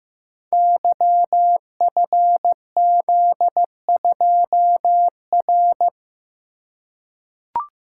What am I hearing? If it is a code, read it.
Morse code, heard as a keyed tone, about 15 wpm, 700 Hz. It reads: YFZ2R
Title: Code Groups - 5 Letter:Numbers - 15wpm